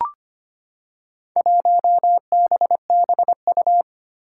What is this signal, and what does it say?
Morse code, heard as a keyed tone, about 25 wpm, 700 Hz. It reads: 1BBU
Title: Four Challenging Letter-Number Combinations 25wpm